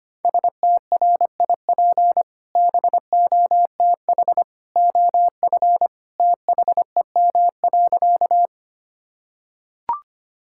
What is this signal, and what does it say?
Morse code, heard as a keyed tone, about 25 wpm, 700 Hz. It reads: STRIP BOTH OF THEM.
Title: Sentences from Top 1600 Words 25wpm